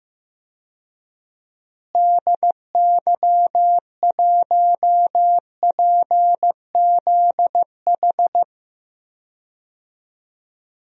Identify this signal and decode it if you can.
Morse code, heard as a keyed tone, about 15 wpm, 700 Hz. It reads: DY1PZH
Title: DXCC-100 Call Signs - 6 Characters - Repeated 3x - 15wpm